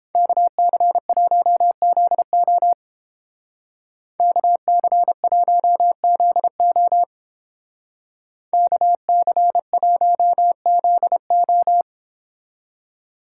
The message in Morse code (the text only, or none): KC1ZO KC1ZO KC1ZO